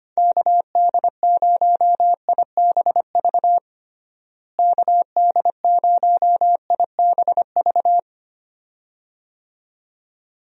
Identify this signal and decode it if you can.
Morse code, heard as a keyed tone, about 25 wpm, 700 Hz. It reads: KD0IBV KD0IBV
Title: USA Call Signs - 6 Characters - Repeated 2x - 25wpm